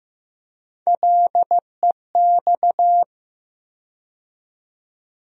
Morse code, heard as a keyed tone, about 15 wpm, 700 Hz. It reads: LEX
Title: USA Call Sign Postfixes - 3 Letters 15wpm